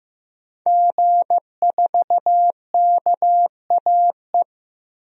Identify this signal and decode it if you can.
Morse code, heard as a keyed tone, about 15 wpm, 700 Hz. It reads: G4KAE
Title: DXCC-100 Call Signs - 5 Characters - Repeated 2x - 15wpm